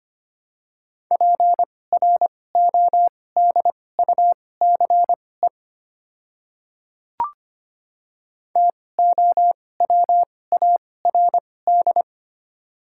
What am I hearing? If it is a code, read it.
Morse code, heard as a keyed tone, about 25 wpm, spaced as for 18 wpm, 690 Hz. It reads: PRODUCE TOWARD
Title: 2x Character Spacing - Top 500 Words - 25wpm